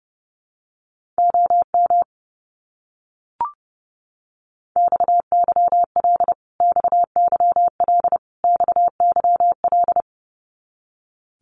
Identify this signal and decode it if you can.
Morse code, heard as a keyed tone, about 30 wpm, 700 Hz. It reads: OM XYL XYL XYL
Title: Abbreviations Repeated 3x - 30wpm